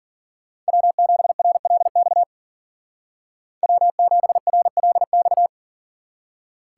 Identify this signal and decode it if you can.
Morse code, heard as a keyed tone, about 49 wpm, 700 Hz. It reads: W7RLX W7RLX
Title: Speed Racing - USA Call Signs 40wpm